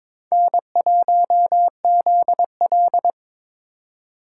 Morse code, heard as a keyed tone, about 26 wpm, 700 Hz. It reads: N1ZL
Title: Speed Racing - USA 1x2 Call Signs 22wpm